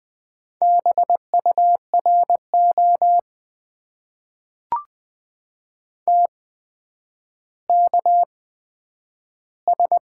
Morse code, heard as a keyed tone, about 20 wpm, spaced as for 10 wpm, 700 Hz. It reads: BURO TKS
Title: Top 90 QSO Elements - 8x Character Spacing - 20wpm